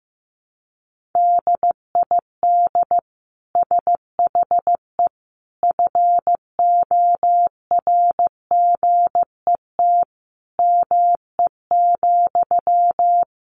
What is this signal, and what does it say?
Morse code, heard as a keyed tone, about 15 wpm, 700 Hz. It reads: DID SHE FORGET ME,
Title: Sentences from Top 1200 Words 15wpm